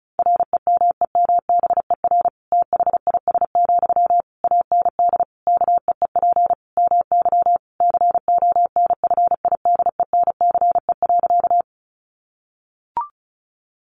Morse code, heard as a keyed tone, about 34 wpm, 700 Hz. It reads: REMEMBER THIS, AND KEEP MY CONFIDENCE.
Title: Sentences from Top 1500 Words 35wpm